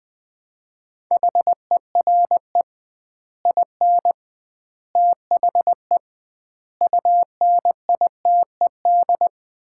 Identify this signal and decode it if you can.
Morse code, heard as a keyed tone, about 20 wpm, 700 Hz. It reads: HERE IN THE UNITED
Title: 2x Word-Spacing - Sets of 5 Words - 20wpm